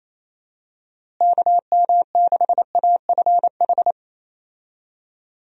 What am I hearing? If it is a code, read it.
Morse code, heard as a keyed tone, about 28 wpm, 700 Hz. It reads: KM6AFH
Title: USA Call Signs - 6 Characters - Repeated 2x - 28wpm